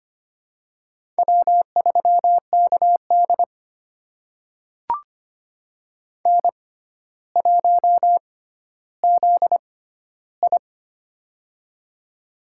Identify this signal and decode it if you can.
Morse code, heard as a keyed tone, about 25 wpm, spaced as for 13 wpm, 700 Hz. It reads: W3KD N1ZI
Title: USA Call Signs - 4 Characters - 6x Character Spacing - 25wpm